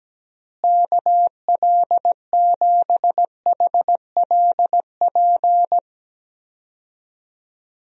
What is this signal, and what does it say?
Morse code, heard as a keyed tone, about 17 wpm, 700 Hz. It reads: KL7HLP
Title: DXCC-100 Call Signs - 6 Characters - Repeated 3x - 17wpm